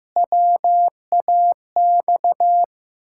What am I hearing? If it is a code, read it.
Morse code, heard as a keyed tone, about 15 wpm, 700 Hz. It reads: WAX
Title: Pangram Challenge v2 - 15wpm